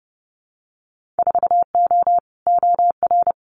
Morse code, heard as a keyed tone, about 30 wpm, 700 Hz. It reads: 4O OR
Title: Winter Field Day Exchanges Encore 30wpm